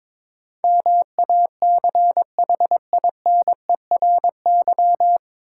Morse code, heard as a keyed tone, about 22 wpm, 700 Hz. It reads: MACHINERY
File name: 9 Letter Words - First 3 Letters Given - 22wpm